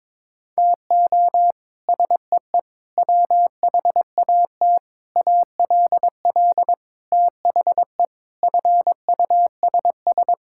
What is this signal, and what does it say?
Morse code, heard as a keyed tone, about 21 wpm, 700 Hz. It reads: TO SEE WHAT ALL THE FUSS
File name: practice / 22wpm / Sets of 8 Words - 22wpm